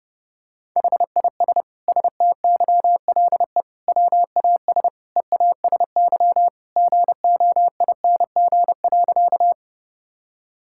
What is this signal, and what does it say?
Morse code, heard as a keyed tone, about 30 wpm, 700 Hz. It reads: HIS STYLE WAS EASY GOING.
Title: Sentences from Top 1000 Words - Review Part 5 - 30wpm